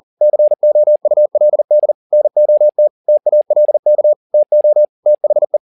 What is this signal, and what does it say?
Morse code, heard as a keyed tone, about 40 wpm, 600 Hz. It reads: COULD NOT TALK TO THE
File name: the old man and the sea 2 40wpm